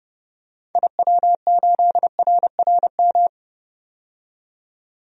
Morse code, heard as a keyed tone, about 30 wpm, 700 Hz. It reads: IW8RRM
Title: DXCC-100 Call Signs - 6 Characters - 30wpm